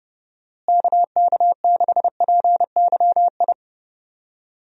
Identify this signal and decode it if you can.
Morse code, heard as a keyed tone, about 30 wpm, 700 Hz. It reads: KK6PYI
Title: USA Call Signs - 6 Characters - Repeated 3x - 30wpm